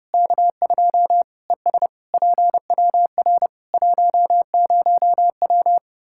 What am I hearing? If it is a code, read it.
Morse code, heard as a keyed tone, about 30 wpm, 700 Hz. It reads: K2 ES PWR 10W
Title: Practice QSO - Exchange 2 - Info - 30wpm